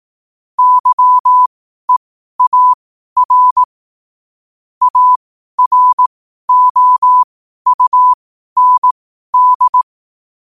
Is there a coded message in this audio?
Yes: Morse code from 0.6 to 9.8 s, heard as a keyed tone, about 18 wpm, spaced as for 12 wpm, 1000 Hz. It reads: YEAR AROUND